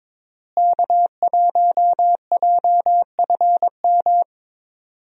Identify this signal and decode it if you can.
Morse code, heard as a keyed tone, about 22 wpm, 700 Hz. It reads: K1JFM
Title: USA Call Signs - 5 Characters - Repeated 2x - 22wpm